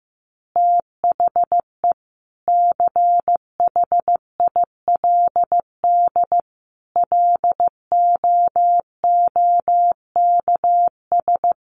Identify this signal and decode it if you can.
Morse code, heard as a keyed tone, about 15 wpm, 700 Hz. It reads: THE CHILD LOOKS
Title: Additional Easy Phrases 15wpm